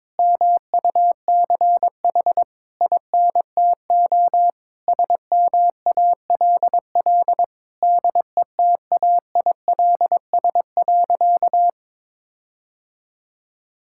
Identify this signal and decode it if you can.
Morse code, heard as a keyed tone, about 21 wpm, 700 Hz. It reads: MUCH INTO SMALL DETAILS.
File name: Sentences with 9-Word Phrases - 22wpm